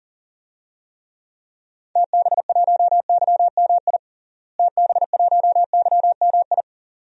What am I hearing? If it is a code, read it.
Morse code, heard as a keyed tone, about 37 wpm, 700 Hz. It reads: TB1YMI TB1YMI
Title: DXCC-100 Call Signs - 6 Characters - Repeated 2x - 40wpm